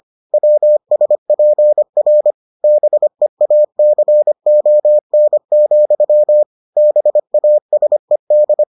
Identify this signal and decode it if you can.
Morse code, heard as a keyed tone, about 25 wpm, 600 Hz. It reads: WSPR BEACON, BASED